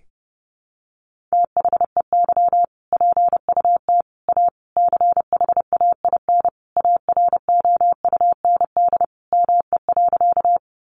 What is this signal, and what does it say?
Morse code, heard as a keyed tone, about 30 wpm, 700 Hz. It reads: THEY PUT A CHAIN AROUND ME.
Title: Sentences from Top 1500 Words 30wpm